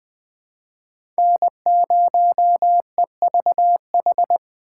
Morse code, heard as a keyed tone, about 21 wpm, 700 Hz. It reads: N0EVH
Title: 2023 SOTA Chaser Call Signs - 20wpm